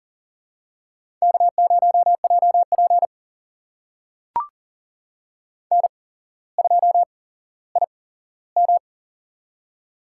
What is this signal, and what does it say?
Morse code, heard as a keyed tone, about 40 wpm, spaced as for 17 wpm, 700 Hz. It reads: K0JP N2IM
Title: USA Call Signs - 8x Character Spacing - 40wpm